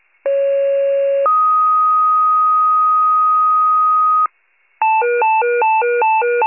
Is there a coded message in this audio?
Yes: two-tone page from 0.3 to 4.3 s, heard as a one-second tone, then a three-second tone, about 550 then 1250 Hz.